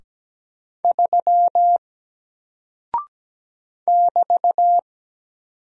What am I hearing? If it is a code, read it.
Morse code, heard as a keyed tone, about 17 wpm, 700 Hz. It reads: 3 <BT>
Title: Single Character (Letter:Number:Punctuation:Prosign) Alt 17wpm